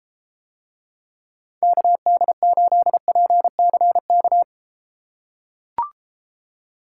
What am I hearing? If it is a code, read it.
Morse code, heard as a keyed tone, about 32 wpm, 700 Hz. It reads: KD8PCK